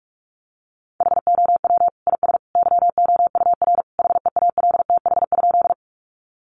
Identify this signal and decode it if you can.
Morse code, heard as a keyed tone, about 44 wpm, 700 Hz. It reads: HOW IS YOUR HEALTH?
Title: Sentences from Top 600 Words 45wpm